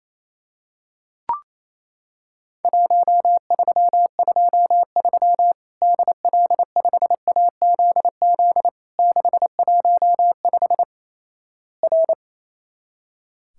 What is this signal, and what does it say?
Morse code, heard as a keyed tone, about 28 wpm, 700 Hz. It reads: 1323 DL5AZZ 615 R
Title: Worked All Europe DX Contest - QTC Exchange - 28wpm